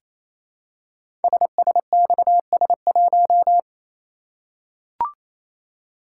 Morse code, heard as a keyed tone, about 28 wpm, 700 Hz. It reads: SSXS1
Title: Code Groups - 5 Letter:Numbers - 28wpm